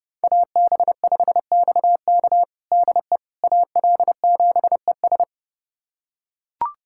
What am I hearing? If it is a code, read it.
Morse code, heard as a keyed tone, about 30 wpm, 700 Hz. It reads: AB5XK DE AL7ES
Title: USA Callsign DE Callsign - 5 Characters - 30wpm